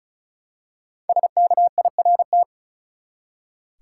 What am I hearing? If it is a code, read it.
Morse code, heard as a keyed tone, about 36 wpm, 700 Hz. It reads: SKIRT
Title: Five Letter Words - 35wpm